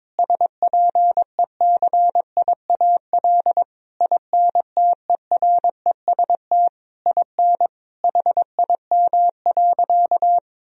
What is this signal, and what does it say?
Morse code, heard as a keyed tone, about 22 wpm, 700 Hz. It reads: SPECIAL INTEREST IN HIM.
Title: Sentences from Top 1000 Words - Review Part 3 - 22wpm